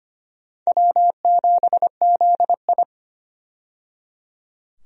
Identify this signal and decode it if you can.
Morse code, heard as a keyed tone, about 25 wpm, 700 Hz. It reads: W7ZI
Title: USA Call Signs - 4 Characters - 25wpm